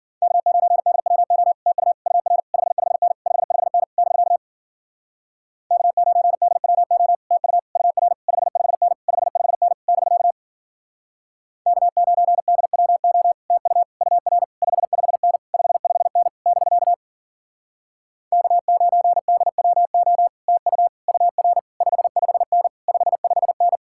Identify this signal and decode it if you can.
Morse code, heard as a keyed tone, about 52 wpm, 700 Hz. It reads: K9DWO TU UR 55N 55N <BK> K9DWO TU UR 55N 55N <BK> K9DWO TU UR 55N 55N <BK> K9DWO TU UR 55N 55N